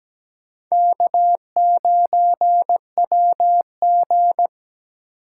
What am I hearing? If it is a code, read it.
Morse code, heard as a keyed tone, about 17 wpm, 700 Hz. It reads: K9WG
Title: USA Call Signs - 4 Characters - 2x Character Spacing - 17wpm